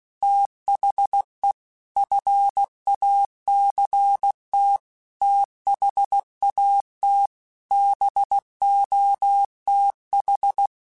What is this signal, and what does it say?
Morse code, heard as a keyed tone, about 16 wpm, 800 Hz. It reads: THE FACT THAT BOTH
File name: Technical Subjects / Technical Subjects-mp3s / Reactance @15